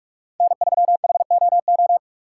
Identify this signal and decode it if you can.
Morse code, heard as a keyed tone, about 42 wpm, 700 Hz. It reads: N3HOO
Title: USA Call Signs - 5 Characters - 4x Character Spacing - 45wpm